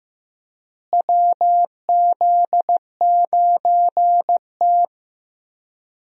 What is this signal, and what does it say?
Morse code, heard as a keyed tone, about 15 wpm, 700 Hz. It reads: WZ9T
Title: USA Call Signs - 4 Characters - Repeated 2x - 15wpm